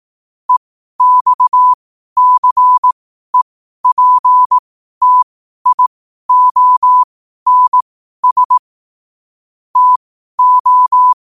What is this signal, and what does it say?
Morse code, heard as a keyed tone, about 18 wpm, spaced as for 12 wpm, 1000 Hz. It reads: EXCEPTIONS TO
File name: qotdfiles / 12wpm